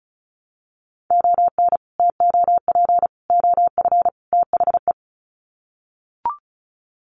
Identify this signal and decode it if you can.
Morse code, heard as a keyed tone, about 34 wpm, 700 Hz. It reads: ON TOP OF THE
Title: Sets of 4 Words 35wpm